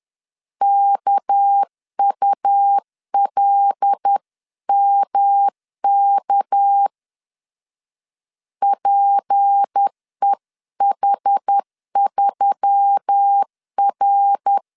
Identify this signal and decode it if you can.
Morse code, heard as a keyed tone, about 10 wpm, 800 Hz. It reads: KULMK PEH3R
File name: LEZIONI / 6 PKX0 / 6 PK 10wpm 2436